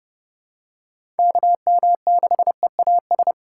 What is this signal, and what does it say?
Morse code, heard as a keyed tone, about 30 wpm, 700 Hz. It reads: KM6EAS
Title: USA Call Signs - 3x Character Spacing - 30wpm